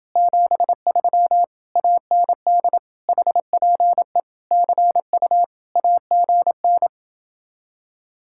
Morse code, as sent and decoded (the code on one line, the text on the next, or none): --... ...-- / .- -. -.. / .... .--. / -.-. ..- / .- --. -.
73 AND HPE CU AGN